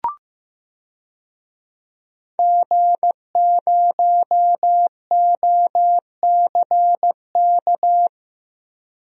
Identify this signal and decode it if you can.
Morse code, heard as a keyed tone, about 15 wpm, 700 Hz. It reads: G0OCK
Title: DXCC-100 Call Signs - 5 Characters - Repeated 2x - 15wpm